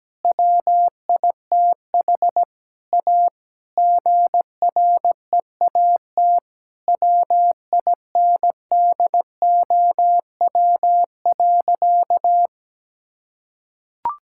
Morse code, heard as a keyed tone, about 17 wpm, 700 Hz. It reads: WITH A GREAT WINDOW.
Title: Sentences from Top 1600 Words 17wpm